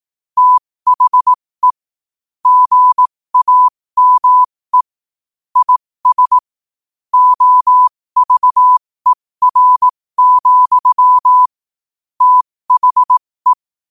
Morse code, heard as a keyed tone, about 18 wpm, spaced as for 15 wpm, 1000 Hz. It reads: THE GAME IS OVER, THE
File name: qotdfiles / 15wpm